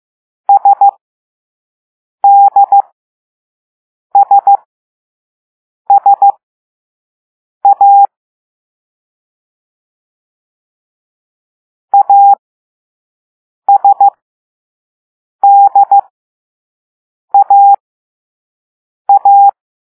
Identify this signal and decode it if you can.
Morse code, heard as a keyed tone, about 15 wpm, spaced as for 5 wpm, 800 Hz. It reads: SDSSA ASDAA